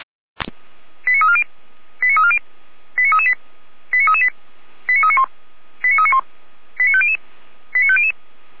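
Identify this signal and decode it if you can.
Five-tone selective call (ZVEI), heard as quick runs of five tones, tones of about 70 ms. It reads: 89340 89340 89338 89338 89391 89391 89500 89500
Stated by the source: ZVEI five-tone call received by SDR.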